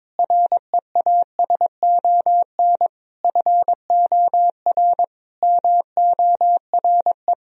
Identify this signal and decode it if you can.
Morse code, heard as a keyed tone, about 21 wpm, 700 Hz. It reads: REASON FOR MORE
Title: Sentences from Top 1000 Words - Review Part 4 - 22wpm